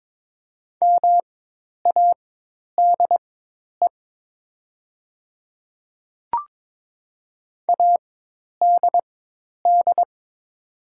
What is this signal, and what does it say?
Morse code, heard as a keyed tone, about 23 wpm, spaced as for 10 wpm, 700 Hz. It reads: MADE ADD
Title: 4x Character Spacing - Top 500 Words - 22wpm